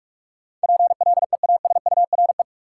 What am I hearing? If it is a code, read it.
Morse code, heard as a keyed tone, about 46 wpm, 700 Hz. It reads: PLEASURE
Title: Sentences from Top 1600 Words - New Word Drill Alt - 45wpm